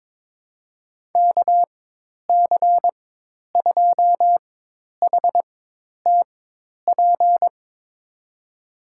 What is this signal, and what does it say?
Morse code, heard as a keyed tone, about 23 wpm, spaced as for 10 wpm, 700 Hz. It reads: KC2HTP
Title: USA Call Signs - 6 Characters - 4x Character Spacing - 22wpm